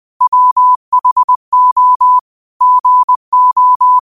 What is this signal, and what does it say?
Morse code, heard as a keyed tone, about 20 wpm, 1000 Hz. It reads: WHO GO
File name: qotdfiles / 20wpm